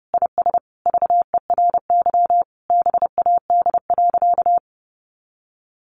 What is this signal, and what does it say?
Morse code, heard as a keyed tone, about 30 wpm, 700 Hz. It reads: IS VERY BAD.
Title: Sentences from Top 1200 Words 30wpm